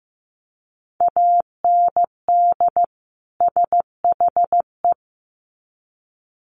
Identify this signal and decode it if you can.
Morse code, heard as a keyed tone, about 15 wpm, 700 Hz. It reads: AND SHE
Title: Sets of 2 Words 15wpm